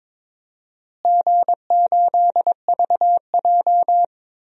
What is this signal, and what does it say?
Morse code, heard as a keyed tone, about 21 wpm, 700 Hz. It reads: G8VJ
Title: DXCC-100 Call Signs - 4 Characters - Repeated 3x - 22wpm